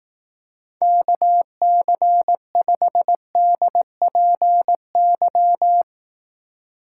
Morse code, heard as a keyed tone, about 18 wpm, 700 Hz. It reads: KC5DPY
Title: Speed Racing - USA 2x3 Call Signs 15wpm